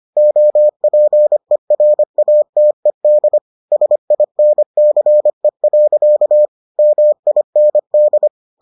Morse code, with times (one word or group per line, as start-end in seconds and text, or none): 0.2-3.4 OPERATED
3.7-6.5 SINCE.
6.8-8.3 MIND